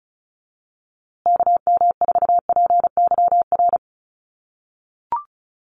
Morse code, heard as a keyed tone, about 34 wpm, 700 Hz. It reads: KM4PYR
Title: USA Call Signs Repeated 2x 35wpm